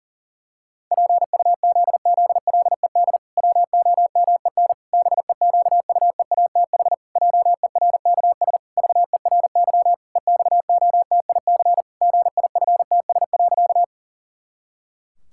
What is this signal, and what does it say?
Morse code, heard as a keyed tone, about 40 wpm, 700 Hz. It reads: PUZZLED WOMEN BEQUEATH JERKS VERY EXOTIC GIFTS.
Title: Pangram Challenge v2 - 40wpm